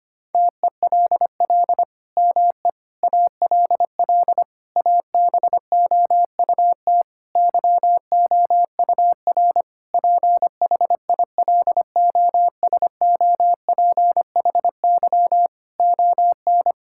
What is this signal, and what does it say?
Morse code, heard as a keyed tone, about 25 wpm, 700 Hz. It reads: TELL ME ALL ABOUT YOUR PHILOSOPHY ON
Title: Sentences from Top 1000 Words - Review Part 6 - 25wpm